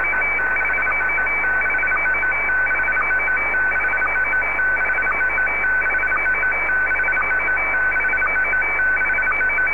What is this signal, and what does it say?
SSTV (slow-scan television), Scottie DX.
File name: sstv / SDX-VK3AOB